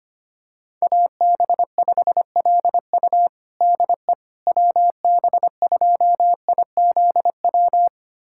Morse code, heard as a keyed tone, about 25 wpm, 700 Hz. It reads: AB5LU DE WB2IZW
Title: USA Callsign DE Callsign 25wpm